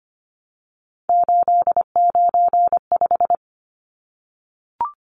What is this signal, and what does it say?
Morse code, heard as a keyed tone, about 25 wpm, 700 Hz. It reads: 895
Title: Numbers - 3-Digit 25wpm